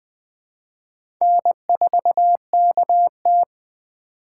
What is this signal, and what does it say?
Morse code, heard as a keyed tone, about 21 wpm, 700 Hz. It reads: N4KT
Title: USA Call Signs - 4 Characters - 2x Character Spacing - 20wpm